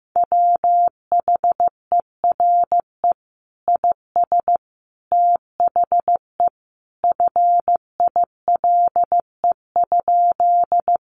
Morse code, heard as a keyed tone, about 15 wpm, 700 Hz. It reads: WHERE IS THE FILE?
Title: Sentences from Top 1200 Words 15wpm